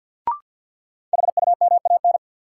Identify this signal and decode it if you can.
Morse code, heard as a keyed tone, about 53 wpm, 700 Hz. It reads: HUMAN
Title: Sets of 5 Words - New Word Drill - 50wpm